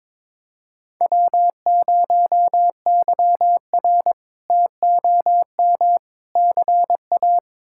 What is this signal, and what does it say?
Morse code, heard as a keyed tone, about 22 wpm, 700 Hz. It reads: W0YR TOM CA
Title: K1USN SST QSO Practice - 22wpm